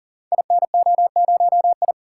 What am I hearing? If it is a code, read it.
Morse code, heard as a keyed tone, about 40 wpm, 690 Hz. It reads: INO0I
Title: Code Groups - 5 Letter:Numbers - 40wpm